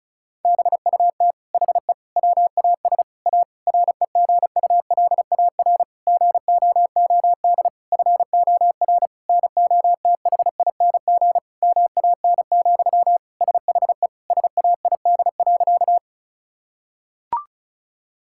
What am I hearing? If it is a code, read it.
Morse code, heard as a keyed tone, about 36 wpm, 700 Hz. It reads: BUT HE WAS A REGULAR GOOD FOR NOTHING MAN, SHE SAID.
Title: Sentences from Top 1000 Words - Review Part 6 - 35wpm